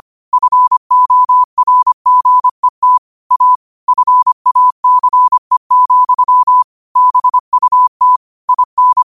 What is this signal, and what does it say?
Morse code, heard as a keyed tone, about 25 wpm, 1000 Hz. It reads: FORGET A FACE, BUT IN